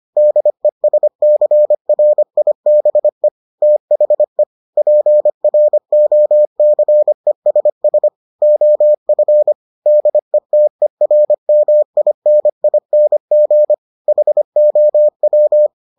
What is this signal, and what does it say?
Morse code, heard as a keyed tone, about 25 wpm, 600 Hz. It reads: DESCRIBE THE PROCESS OF DETERMINING HOW